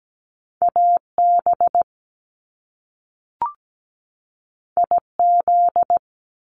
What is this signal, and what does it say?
Morse code, heard as a keyed tone, about 17 wpm, 700 Hz. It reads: AB IZ
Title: Top Two Letter Combinations in Words 17wpm